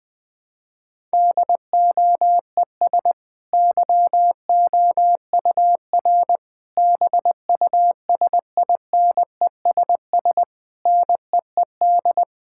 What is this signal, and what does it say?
Morse code, heard as a keyed tone, about 20 wpm, 700 Hz. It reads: DOES YOUR BUSINESS NEED
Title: Sentences from Top 1000 Words - Review Part 6 - 20wpm